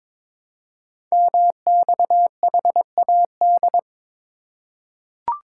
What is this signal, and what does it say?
Morse code, heard as a keyed tone, about 22 wpm, 700 Hz. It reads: MXHAD
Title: Code Groups - 5 Letters - 22wpm